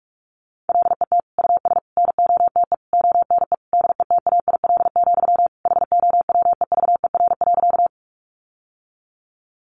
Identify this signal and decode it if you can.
Morse code, heard as a keyed tone, about 44 wpm, 700 Hz. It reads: LET US NOTE ONE DETAIL, HOWEVER.
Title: Sentences from Top 1500 Words 45wpm